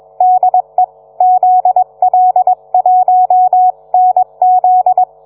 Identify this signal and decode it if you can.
Morse code, heard as a keyed tone, about 21 wpm, 740 Hz. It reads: DE ZL1NZ